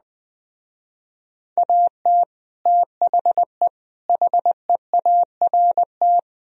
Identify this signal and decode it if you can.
Morse code, heard as a keyed tone, about 21 wpm, 700 Hz. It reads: AT THE HEART